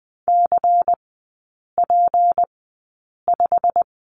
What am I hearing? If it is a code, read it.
Morse code, heard as a keyed tone, about 20 wpm, spaced as for 8 wpm, 700 Hz. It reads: CP5
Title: 152 Lesson 25e - Letter p - Three Characters 20wpm